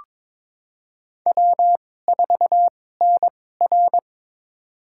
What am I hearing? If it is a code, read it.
Morse code, heard as a keyed tone, about 22 wpm, spaced as for 16 wpm, 700 Hz. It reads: W4NR
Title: USA Call Signs - 2x Character Spacing - 22wpm